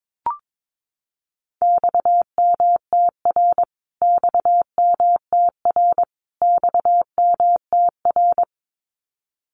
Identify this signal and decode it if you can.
Morse code, heard as a keyed tone, about 22 wpm, 700 Hz. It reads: XMTR XMTR XMTR
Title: Abbreviations Repeated 3x - 22wpm